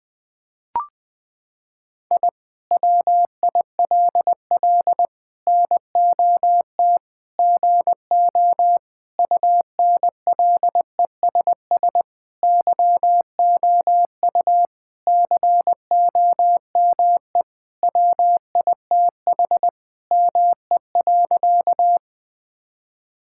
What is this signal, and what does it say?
Morse code, heard as a keyed tone, about 20 wpm, 700 Hz. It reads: I WILL NOT GO UNLESS YOU COME WITH ME.
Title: Sentences from Top 1000 Words - Review Part 6 - 20wpm